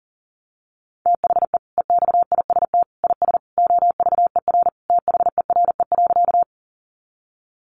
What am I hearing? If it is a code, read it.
Morse code, heard as a keyed tone, about 40 wpm, 700 Hz. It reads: THE EXIST IS OVER THERE.
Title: Sentences from Top 1100 Words 40wpm